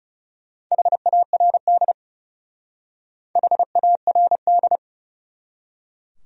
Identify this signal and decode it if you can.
Morse code, heard as a keyed tone, about 37 wpm, 700 Hz. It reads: HARD HARD
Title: Speed-Racing Top 300 Words 30wpm